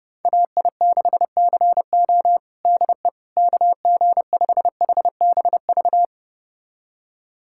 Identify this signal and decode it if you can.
Morse code, heard as a keyed tone, about 30 wpm, 700 Hz. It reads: AI6CO DE KG5HBV
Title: USA Callsign DE Callsign 30wpm